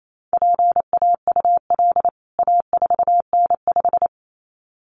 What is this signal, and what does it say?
Morse code, heard as a keyed tone, about 28 wpm, 700 Hz. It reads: PAUL A4N5
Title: CWT Exchange 28wpm